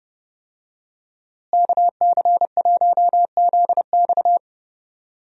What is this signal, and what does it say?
Morse code, heard as a keyed tone, about 31 wpm, 700 Hz. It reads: KC1ZX
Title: USA Call Signs - 5 Characters - Repeated 2x - 30wpm